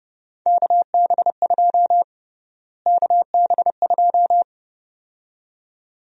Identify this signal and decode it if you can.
Morse code, heard as a keyed tone, about 32 wpm, 700 Hz. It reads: KB2 KB2
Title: USA Call Sign Prefixes - Repeated 2x - 30wpm